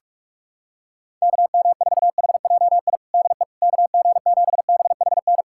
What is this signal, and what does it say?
Morse code, heard as a keyed tone, about 46 wpm, 700 Hz. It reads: KM4HJI DE KG7BHN
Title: USA Callsign DE Callsign - 6 Characters - 45wpm